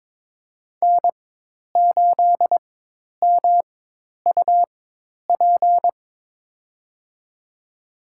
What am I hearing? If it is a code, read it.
Morse code, heard as a keyed tone, about 21 wpm, spaced as for 10 wpm, 700 Hz. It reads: N8MUP
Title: USA Call Signs - 5 Characters - 4x Character Spacing - 22wpm